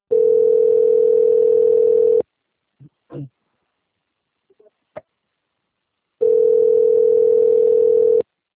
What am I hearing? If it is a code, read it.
Ringback tone (North America), 440 + 480 Hz.